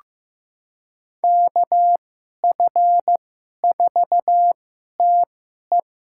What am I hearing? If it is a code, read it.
Morse code, heard as a keyed tone, about 15 wpm, spaced as for 11 wpm, 700 Hz. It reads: KF4TE
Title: USA Call Signs - 5 Characters - 2x Character Spacing - 15wpm